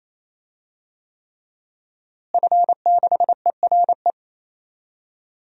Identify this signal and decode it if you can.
Morse code, heard as a keyed tone, about 27 wpm, 700 Hz. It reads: F6ERE
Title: DXCC-100 Call Signs - 5 Characters - Repeated 3x - 28wpm